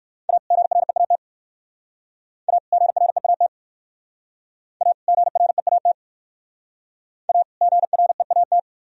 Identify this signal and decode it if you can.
Morse code, heard as a keyed tone, about 53 wpm, 700 Hz. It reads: A GREAT A GREAT A GREAT A GREAT
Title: Speed-Racing - Sets of 2 Words 45wpm